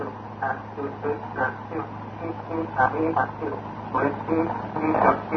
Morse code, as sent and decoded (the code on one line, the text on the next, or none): none